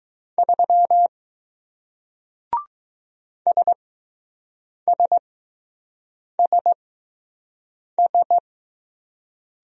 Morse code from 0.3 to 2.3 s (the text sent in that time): 3